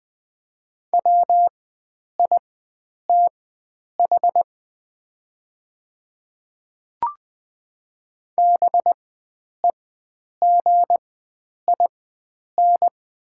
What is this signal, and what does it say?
Morse code, heard as a keyed tone, about 20 wpm, spaced as for 9 wpm, 700 Hz. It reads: WITH BEGIN
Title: 4x Character Spacing - Top 300 Words - 20wpm